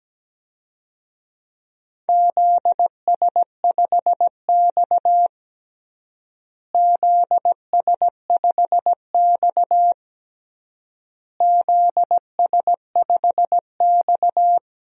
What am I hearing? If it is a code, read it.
Morse code, heard as a keyed tone, about 17 wpm, 700 Hz. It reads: ZS5X ZS5X ZS5X
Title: DXCC-100 Call Signs - 4 Characters - Repeated 3x - 17wpm